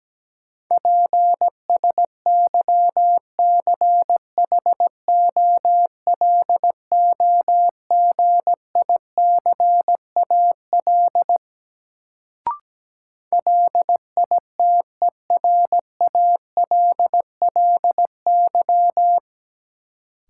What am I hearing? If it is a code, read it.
Morse code, heard as a keyed tone, about 17 wpm, 700 Hz. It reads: PSYCHOLOGICAL LITERALLY
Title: Sentences from Top 1400 Words - New Word Drill - 17wpm